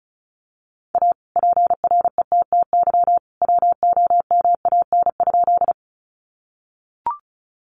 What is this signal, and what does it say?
Morse code, heard as a keyed tone, about 34 wpm, 700 Hz. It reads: A PRETTY WOMAN?
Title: Sentences from Top 1100 Words 35wpm